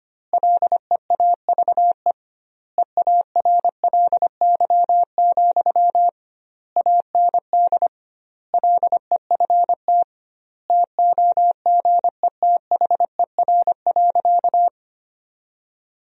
Morse code, heard as a keyed tone, about 25 wpm, 700 Hz. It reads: LEAVE EARLY, AND LEFT TOGETHER.
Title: 2x Word-Spacing - Sentences from Top 400 Words - 25wpm